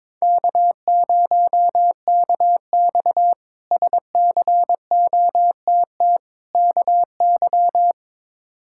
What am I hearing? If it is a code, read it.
Morse code, heard as a keyed tone, about 21 wpm, 700 Hz. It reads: K0KX SCOTT KY
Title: K1USN SST QSO Practice - 22wpm